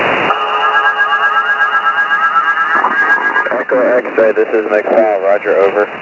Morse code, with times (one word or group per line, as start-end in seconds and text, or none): none